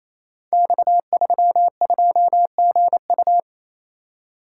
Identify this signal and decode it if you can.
Morse code, heard as a keyed tone, about 27 wpm, 700 Hz. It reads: X32GU
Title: Code Groups - 5 Letter:Numbers - 28wpm